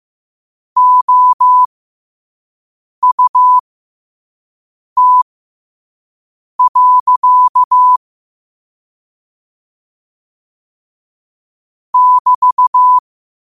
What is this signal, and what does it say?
Morse code, heard as a keyed tone, about 15 wpm, spaced as for 5 wpm, 1000 Hz. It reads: OUT. <BT>
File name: qotdfiles / 5wpm